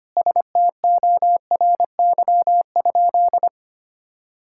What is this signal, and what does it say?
Morse code, heard as a keyed tone, about 25 wpm, 700 Hz. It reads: STORY?
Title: Sentences from Top 1000 Words - Review Part 5 - 25wpm